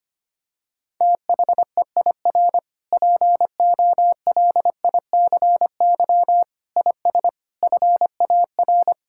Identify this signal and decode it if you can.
Morse code, heard as a keyed tone, about 25 wpm, 700 Hz. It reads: THEIR POLICY IS FAR
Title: Sentences from Top 1000 Words - Review Part 5 - 25wpm